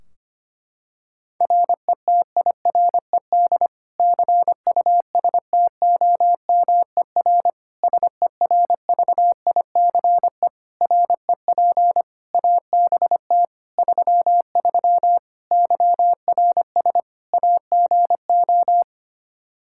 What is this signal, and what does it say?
Morse code, heard as a keyed tone, about 25 wpm, 700 Hz. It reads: RETIRED CUSTOMER SERVICE REP ABT 33 YRS AGO
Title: Practice QSO - Exchange 3 - More Info - 25wpm